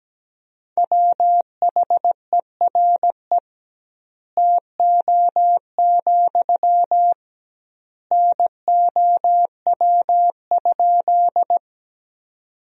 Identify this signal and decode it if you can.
Morse code, heard as a keyed tone, about 17 wpm, 700 Hz. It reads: WHERE TO, NOW?
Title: 2x Word-Spacing - Sentences from Top 200 Words - 17wpm